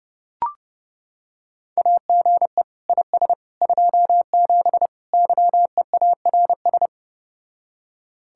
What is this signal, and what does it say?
Morse code, heard as a keyed tone, about 30 wpm, 700 Hz. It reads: AGE IS 27 YEARS